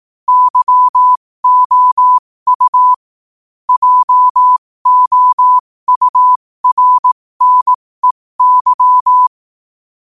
Morse code, heard as a keyed tone, about 18 wpm, spaced as for 15 wpm, 1000 Hz. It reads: YOU JOURNEY
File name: qotdfiles / 15wpm